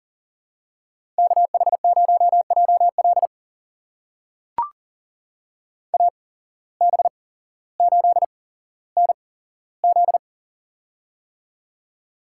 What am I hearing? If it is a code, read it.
Morse code, heard as a keyed tone, about 40 wpm, spaced as for 17 wpm, 700 Hz. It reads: KH0JL AB8NZ
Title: USA Call Signs - 5 Characters - 8x Character Spacing - 40wpm